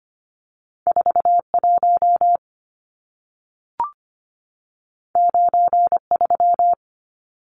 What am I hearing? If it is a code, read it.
Morse code, heard as a keyed tone, about 25 wpm, 700 Hz. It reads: 41 93
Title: Numbers - 2-Digit 25wpm